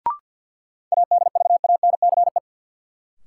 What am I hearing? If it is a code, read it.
Morse code, heard as a keyed tone, about 52 wpm, 700 Hz. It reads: ADVANCE
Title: Sentences from Top 1400 Words - New Word Drill Alt - 50wpm